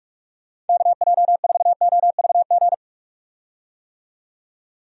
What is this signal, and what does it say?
Morse code, heard as a keyed tone, about 44 wpm, 700 Hz. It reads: KJ4OVG
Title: USA Call Signs - 6 Characters - 45wpm